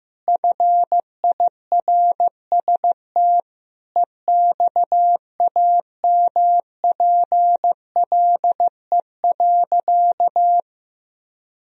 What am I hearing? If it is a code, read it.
Morse code, heard as a keyed tone, about 16 wpm, 700 Hz. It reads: FIRST EXAMPLE.
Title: Sentences from Top 1000 Words - Review Part 1 - 15wpm